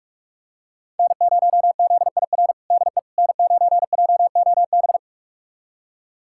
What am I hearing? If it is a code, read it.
Morse code, heard as a keyed tone, about 41 wpm, 700 Hz. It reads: N0ZIR DE N9JOB
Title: USA Callsign DE Callsign - 5 Characters - 45wpm